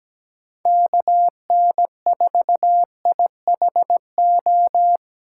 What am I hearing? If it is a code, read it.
Morse code, heard as a keyed tone, about 17 wpm, 700 Hz. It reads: KN4IHO
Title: USA Call Signs - 6 Characters - 3x Character Spacing - 17wpm